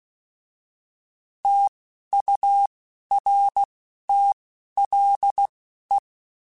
Morse code, heard as a keyed tone, about 16 wpm, spaced as for 11 wpm, 800 Hz. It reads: TURTLE